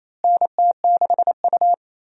Morse code, heard as a keyed tone, about 27 wpm, 700 Hz. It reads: NT6U
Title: USA Call Signs - 4 Characters - Repeated 2x - 28wpm